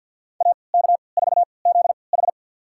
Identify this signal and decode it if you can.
Morse code, heard as a keyed tone, about 49 wpm, spaced as for 28 wpm, 700 Hz. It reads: AK4ZH
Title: USA Call Signs - 3x Character Spacing - 50wpm